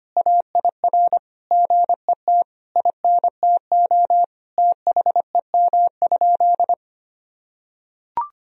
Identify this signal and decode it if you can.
Morse code, heard as a keyed tone, about 25 wpm, 700 Hz. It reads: AIR GET INTO THEM?
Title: Sentences from Top 1000 Words - Review Part 1 - 25wpm